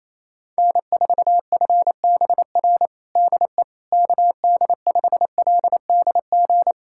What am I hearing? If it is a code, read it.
Morse code, heard as a keyed tone, about 28 wpm, 700 Hz. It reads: N4FBR DE KD5LDG
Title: USA Callsign DE Callsign 28wpm